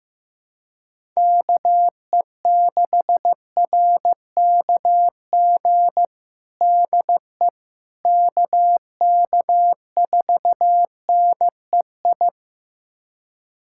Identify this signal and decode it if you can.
Morse code, heard as a keyed tone, about 15 wpm, 700 Hz. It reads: KE6RKG DE KK4NEI
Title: USA Callsign DE Callsign 15wpm